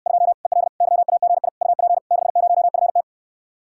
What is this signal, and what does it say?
Morse code, heard as a keyed tone, about 69 wpm, 700 Hz. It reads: 2 EL YAGI UP 60FT